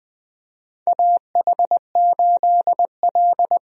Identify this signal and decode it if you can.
Morse code, heard as a keyed tone, about 20 wpm, 700 Hz. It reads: AH8L
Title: USA Call Signs - 4 Characters - 20wpm